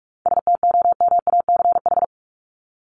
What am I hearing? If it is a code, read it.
Morse code, heard as a keyed tone, about 44 wpm, 700 Hz. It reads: STOMACH
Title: Top 1501-1750 Words 45wpm